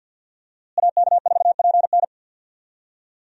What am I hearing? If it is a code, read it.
Morse code, heard as a keyed tone, about 52 wpm, 700 Hz. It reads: AK4PN
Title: USA Call Signs - 5 Characters - Repeated 2x - 50wpm